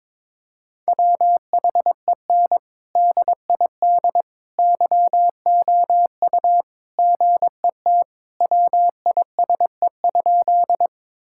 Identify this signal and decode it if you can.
Morse code, heard as a keyed tone, about 22 wpm, 700 Hz. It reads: WHEN DID YOU GET WISE?
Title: Sentences from Top 1600 Words 22wpm